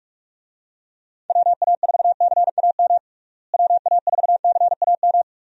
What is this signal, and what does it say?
Morse code, heard as a keyed tone, about 46 wpm, 700 Hz. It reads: WA4CAM WA4CAM
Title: USA Call Signs - 6 Characters - Repeated 2x - 45wpm